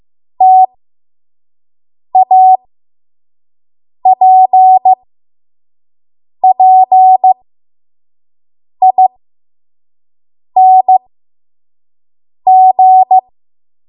Morse code, heard as a keyed tone, about 15 wpm, spaced as for 5 wpm, 750 Hz. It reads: TAPPING
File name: Morse / 05WPM